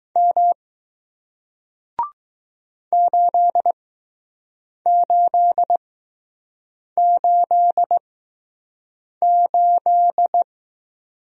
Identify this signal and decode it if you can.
Morse code, heard as a keyed tone, about 20 wpm, spaced as for 6 wpm, 700 Hz. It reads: M 8888